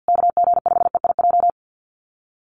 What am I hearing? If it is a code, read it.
Morse code, heard as a keyed tone, about 50 wpm, 700 Hz. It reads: KG5EIJ
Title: USA Call Signs Repeated 3x 50wpm